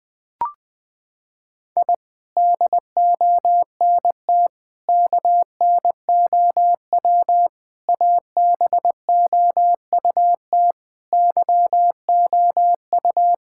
Morse code, heard as a keyed tone, about 20 wpm, 700 Hz. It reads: I DONT KNOW ABOUT YOU
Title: Sentences with 6-Word Phrases - 20wpm